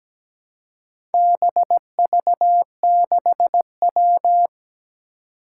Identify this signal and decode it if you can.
Morse code, heard as a keyed tone, about 17 wpm, 700 Hz. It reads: BV6W
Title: Four Challenging Letter-Number Combinations 17wpm